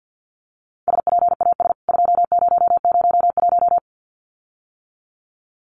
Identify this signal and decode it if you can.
Morse code, heard as a keyed tone, about 50 wpm, 700 Hz. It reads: SPAS 2001
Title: CWT Exchange 50wpm